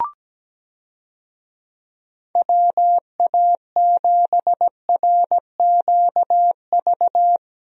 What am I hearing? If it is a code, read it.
Morse code, heard as a keyed tone, about 17 wpm, 700 Hz. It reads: WA7RQV
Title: USA Call Signs - 6 Characters - Repeated 3x - 17wpm